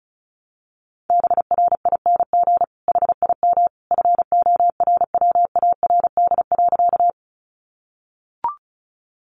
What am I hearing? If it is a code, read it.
Morse code, heard as a keyed tone, about 34 wpm, 700 Hz. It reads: BRING HIM FORWARD.
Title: Sentences from Top 1300 Words 35wpm